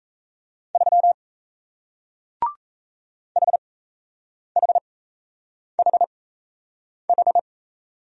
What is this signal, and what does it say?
Morse code, heard as a keyed tone, about 38 wpm, spaced as for 8 wpm, 700 Hz. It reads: 3 HHHH